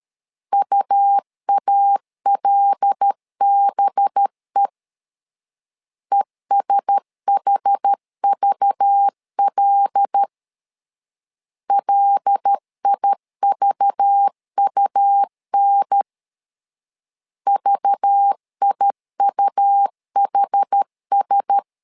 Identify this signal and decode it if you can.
Morse code, heard as a keyed tone, about 13 wpm, 800 Hz. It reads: UALBE ESHVL LIVUN VIUHS
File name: LEZIONI / 5 LBVC / 5 LBV 12wpm 1855